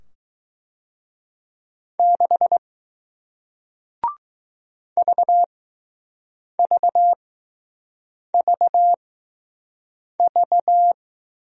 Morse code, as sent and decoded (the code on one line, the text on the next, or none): -.... / ...- ...- ...- ...-
6 VVVV